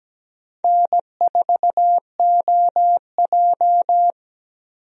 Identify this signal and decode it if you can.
Morse code, heard as a keyed tone, about 17 wpm, 700 Hz. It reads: N4OJ
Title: USA Call Signs - 4 Characters - 17wpm